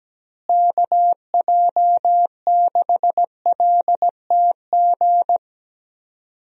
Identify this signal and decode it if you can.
Morse code, heard as a keyed tone, about 17 wpm, 700 Hz. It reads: KJ6LTG
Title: USA Call Signs - 6 Characters - Repeated 2x - 17wpm